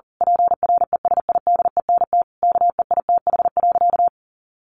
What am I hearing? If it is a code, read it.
Morse code, heard as a keyed tone, about 40 wpm, 700 Hz. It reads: PRESIDENT KEITH.
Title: A Sound Of Thunder 40wpm